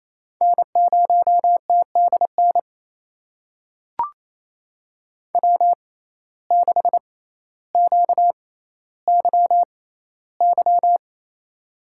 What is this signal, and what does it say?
Morse code, heard as a keyed tone, about 28 wpm, spaced as for 14 wpm, 700 Hz. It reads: N0TDN W6QYY
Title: USA Call Signs - 5 Characters - 6x Character Spacing - 28wpm